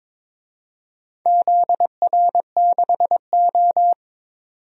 Morse code, heard as a keyed tone, about 21 wpm, 700 Hz. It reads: ZR6O
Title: DXCC-100 Call Signs - 4 Characters - 22wpm